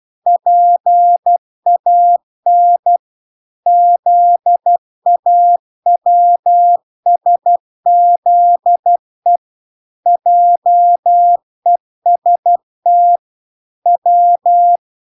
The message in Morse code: PAN ZAWSZE JEST W

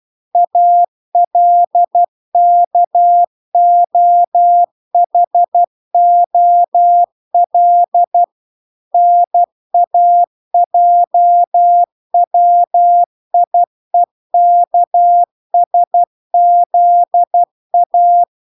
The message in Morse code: ALKOHOL NAJWIEKSZA

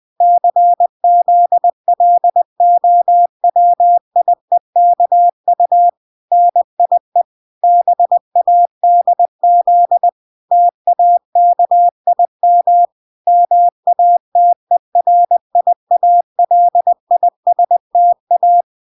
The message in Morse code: CZLOWIEKU NIE BADZ TAKIM MATERIALISTA

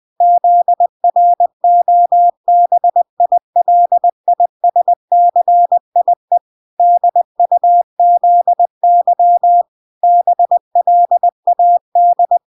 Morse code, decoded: ZROBILISCIE DUZY BLAD

